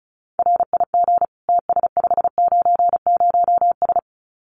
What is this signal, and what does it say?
Morse code, heard as a keyed tone, about 36 wpm, 700 Hz. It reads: RIG TS590S
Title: Common QSO Phrases Encore 35wpm